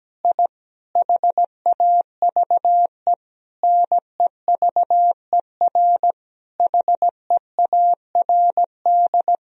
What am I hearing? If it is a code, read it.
Morse code, heard as a keyed tone, about 17 wpm, 700 Hz. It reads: I HAVE NEVER HEARD
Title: Sentences from Top 1000 Words - Review Part 3 - 17wpm